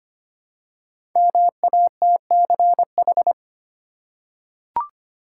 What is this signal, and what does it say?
Morse code, heard as a keyed tone, about 25 wpm, 700 Hz. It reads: MATCH
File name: 5 Letter Words - First 3 Letters Given - 25wpm